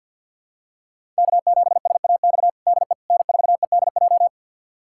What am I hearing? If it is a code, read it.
Morse code, heard as a keyed tone, about 51 wpm, 700 Hz. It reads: K7SAX DE N4EDJ
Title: USA Callsign DE Callsign - 5 Characters - 50wpm